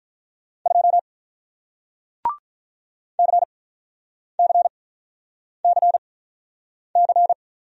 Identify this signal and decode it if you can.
Morse code, heard as a keyed tone, about 48 wpm, spaced as for 9 wpm, 700 Hz. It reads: 2 CCCC